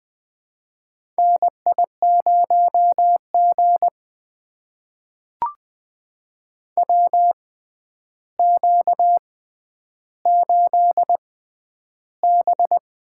Morse code, heard as a keyed tone, about 20 wpm, spaced as for 10 wpm, 700 Hz. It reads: NI0G WQ8B
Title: USA Call Signs - 4 Characters - 6x Character Spacing - 20wpm